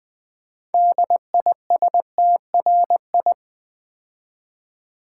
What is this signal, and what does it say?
Morse code, heard as a keyed tone, about 21 wpm, 700 Hz. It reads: DISTRI
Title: Build A Word - 3 Letters At A Time - 8-Letter Words - 20wpm